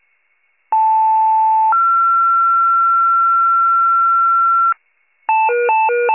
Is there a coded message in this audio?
Yes: two-tone page from 0.7 to 4.7 s, heard as a one-second tone, then a three-second tone, about 880 then 1360 Hz.